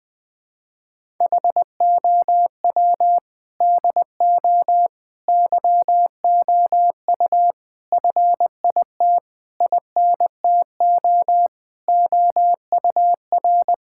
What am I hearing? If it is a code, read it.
Morse code, heard as a keyed tone, about 21 wpm, 700 Hz. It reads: HOW DO YOU FIT INTO OUR
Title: Sentences from Top 1000 Words - Review Part 6 - 20wpm